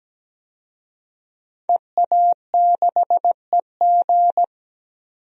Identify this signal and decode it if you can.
Morse code, heard as a keyed tone, about 17 wpm, 700 Hz. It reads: EA6EG
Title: DXCC-100 Call Signs - 5 Characters - 17wpm